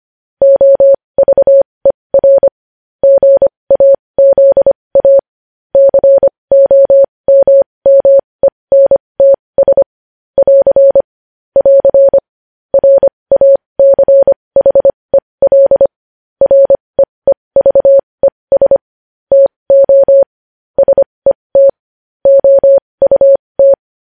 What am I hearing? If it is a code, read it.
Morse code, heard as a keyed tone, about 25 wpm, spaced as for 20 wpm, 550 Hz. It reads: OVER GAZA COMMENTS <AR> <AR> RACHEL REEVES TO SET OUT
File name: podcast / News headlines-1x-25x20